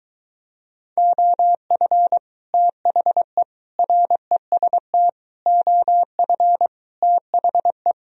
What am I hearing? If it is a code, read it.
Morse code, heard as a keyed tone, about 23 wpm, 700 Hz. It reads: OF THE REST OF THE